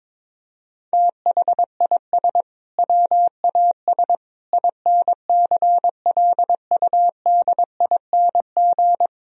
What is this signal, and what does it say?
Morse code, heard as a keyed tone, about 22 wpm, 700 Hz. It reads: THIS WAS INCLUDING
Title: Sentences from Top 1000 Words - Review Part 4 - 22wpm